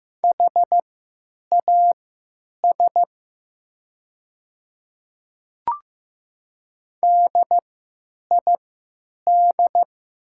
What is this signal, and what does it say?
Morse code, heard as a keyed tone, about 15 wpm, spaced as for 9 wpm, 700 Hz. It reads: HAS DID
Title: 3x Character Spacing - Top 100 Words - 15wpm